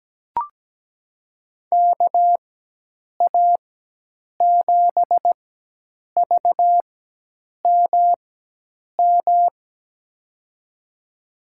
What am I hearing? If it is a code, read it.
Morse code, heard as a keyed tone, about 17 wpm, spaced as for 8 wpm, 700 Hz. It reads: KA7VMM
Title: USA Call Signs - 6 Characters - 4x Character Spacing - 17wpm